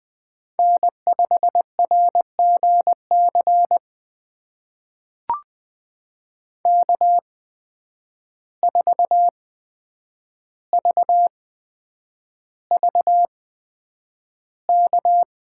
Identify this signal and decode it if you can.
Morse code, heard as a keyed tone, about 20 wpm, spaced as for 9 wpm, 700 Hz. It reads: N5RGC K4VVK
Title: USA Call Signs - 5 Characters - 8x Character Spacing - 20wpm